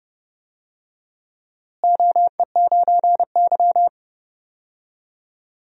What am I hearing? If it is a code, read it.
Morse code, heard as a keyed tone, about 30 wpm, 700 Hz. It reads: OE9Y
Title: DXCC-100 Call Signs - 4 Characters - Repeated 2x - 30wpm